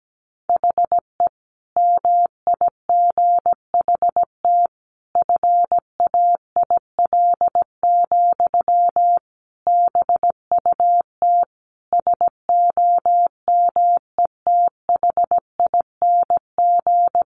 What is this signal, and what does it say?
Morse code, heard as a keyed tone, about 17 wpm, 700 Hz. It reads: HE MIGHT FAIL, BUT SOMETHING
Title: Sentences from Top 1100 Words 17wpm